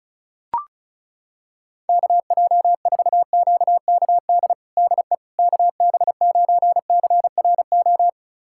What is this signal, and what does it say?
Morse code, heard as a keyed tone, about 36 wpm, 700 Hz. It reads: KJ4QKD DE KB9CRO
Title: USA Callsign DE Callsign - 6 Characters - 35wpm